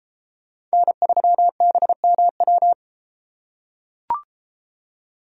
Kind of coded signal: Morse code

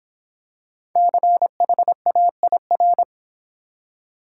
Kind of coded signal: Morse code

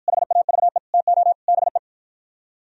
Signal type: Morse code

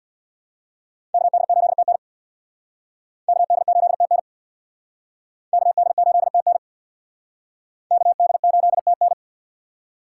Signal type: Morse code

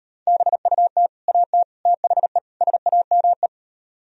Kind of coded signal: Morse code

